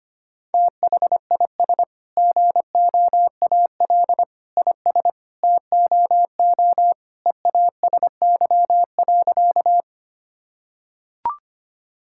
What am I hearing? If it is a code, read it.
Morse code, heard as a keyed tone, about 25 wpm, 700 Hz. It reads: THIS GOAL IS TOO EASY.
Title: Sentences from Top 1000 Words - Review Part 5 - 25wpm